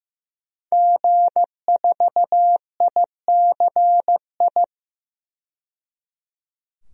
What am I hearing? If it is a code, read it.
Morse code, heard as a keyed tone, about 15 wpm, 700 Hz. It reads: G4ICI
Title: DXCC-100 Call Signs - 5 Characters - 15wpm